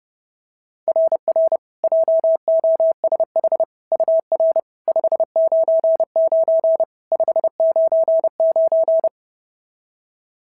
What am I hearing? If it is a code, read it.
Morse code, heard as a keyed tone, about 30 wpm, 690 Hz. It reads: RR JOSH UR 599 599
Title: SOTA QSO Practice 30wpm